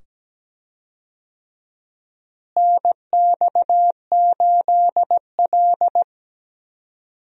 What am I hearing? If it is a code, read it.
Morse code, heard as a keyed tone, about 17 wpm, 700 Hz. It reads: NX8L
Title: USA Call Signs - 4 Characters - Repeated 2x - 17wpm